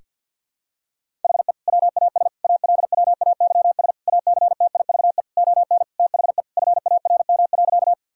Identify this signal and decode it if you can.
Morse code, heard as a keyed tone, about 51 wpm, 700 Hz. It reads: HE WAS ALWAYS ACTIVE ON THE FARM.
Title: Sentences from Top 1000 Words - Review Part 6 - 50wpm